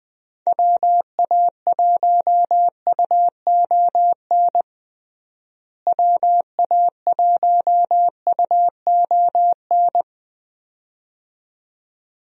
Morse code, heard as a keyed tone, about 20 wpm, 700 Hz. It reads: WA1UON WA1UON
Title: USA Call Signs - 6 Characters - Repeated 2x - 20wpm